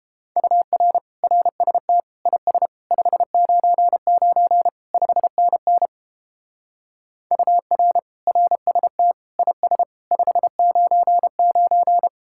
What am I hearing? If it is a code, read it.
Morse code, heard as a keyed tone, about 29 wpm, 700 Hz. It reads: UR RST IS 599 5NN UR RST IS 599